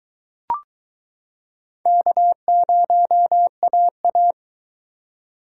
Morse code, heard as a keyed tone, about 23 wpm, 700 Hz. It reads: K0AA